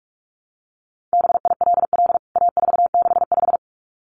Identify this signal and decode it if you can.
Morse code, heard as a keyed tone, about 44 wpm, 700 Hz. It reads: BILL A465
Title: CWT Exchange 45wpm